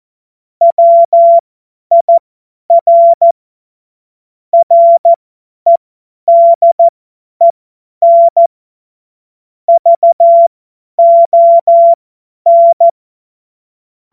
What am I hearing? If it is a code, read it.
Morse code, heard as a keyed tone, about 14 wpm, spaced as for 10 wpm, 680 Hz. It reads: WIR REDEN VON
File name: Morsekurs / Witze / Witz 21. Juni 2022 50 F70